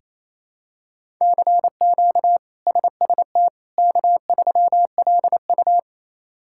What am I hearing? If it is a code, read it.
Morse code, heard as a keyed tone, about 28 wpm, 700 Hz. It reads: CQ SST K3LU
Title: K1USN SST QSO Practice - 28wpm